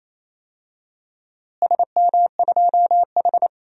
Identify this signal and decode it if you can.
Morse code, heard as a keyed tone, about 28 wpm, 700 Hz. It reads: SM2H
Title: DXCC-100 Call Signs - 4 Characters - Repeated 3x - 28wpm